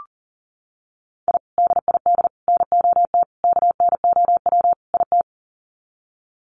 Morse code, heard as a keyed tone, about 38 wpm, 700 Hz. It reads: I DID NOT KNOW IT
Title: Sets of 5 Words Encore2 40wpm